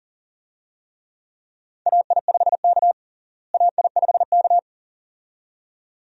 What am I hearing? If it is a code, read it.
Morse code, heard as a keyed tone, about 40 wpm, 700 Hz. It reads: AI5K AI5K
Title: USA Call Signs - 4 Characters - Repeated 2x - 40wpm